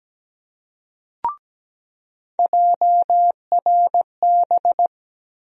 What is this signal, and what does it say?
Morse code, heard as a keyed tone, about 17 wpm, 700 Hz. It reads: JRB
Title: USA Call Sign Postfixes - 3 Letters 17wpm